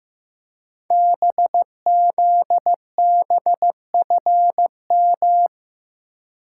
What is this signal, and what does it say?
Morse code, heard as a keyed tone, about 15 wpm, 700 Hz. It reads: BZBFM
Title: Code Groups - 5 Letters - 15wpm